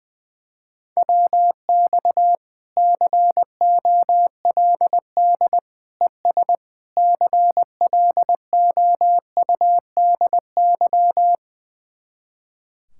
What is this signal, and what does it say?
Morse code, heard as a keyed tone, about 20 wpm, 700 Hz. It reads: WX COLD ES CLOUDY
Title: Speed-Racing Common QSO Phrases - 20wpm